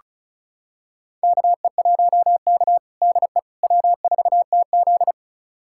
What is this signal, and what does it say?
Morse code, heard as a keyed tone, about 36 wpm, 700 Hz. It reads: KE1K DE W4TZ
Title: USA Callsign DE Callsign - 4 Characters - 35wpm